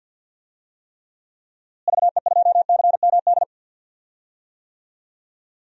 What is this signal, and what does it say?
Morse code, heard as a keyed tone, about 52 wpm, 700 Hz. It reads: VE2CMD
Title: DXCC-100 Call Signs - 6 Characters - Repeated 3x - 50wpm